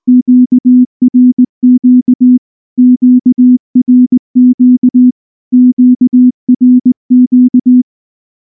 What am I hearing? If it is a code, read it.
Morse code, heard as a keyed tone, about 18 wpm, 250 Hz. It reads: QRQ QRQ QRQ